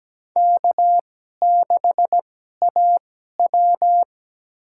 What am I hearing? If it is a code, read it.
Morse code, heard as a keyed tone, about 17 wpm, spaced as for 12 wpm, 700 Hz. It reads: K6AW
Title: USA Call Signs - 4 Characters - 2x Character Spacing - 17wpm